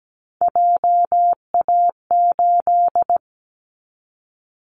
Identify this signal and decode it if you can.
Morse code, heard as a keyed tone, about 17 wpm, 700 Hz. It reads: JA8
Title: Callsign Prefixes for Top 50 DXCC - 17wpm